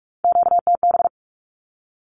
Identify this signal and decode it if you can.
Morse code, heard as a keyed tone, about 44 wpm, 700 Hz. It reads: QTB
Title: Q-Codes 45wpm